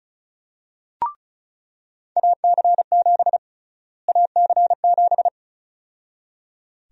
Morse code, heard as a keyed tone, about 36 wpm, 700 Hz. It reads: AC7 AC7
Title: USA Call Sign Prefixes - Repeated 2x - 35wpm